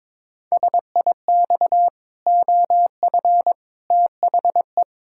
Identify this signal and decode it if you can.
Morse code, heard as a keyed tone, about 22 wpm, 700 Hz. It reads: SIX OF THE
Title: Pangram Challenge v2 - 22wpm